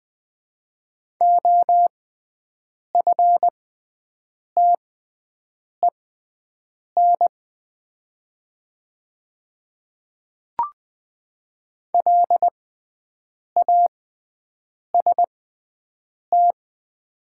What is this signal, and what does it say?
Morse code, heard as a keyed tone, about 20 wpm, spaced as for 7 wpm, 700 Hz. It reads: OFTEN LAST
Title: 6x Character Spacing - Top 300 Words - 20wpm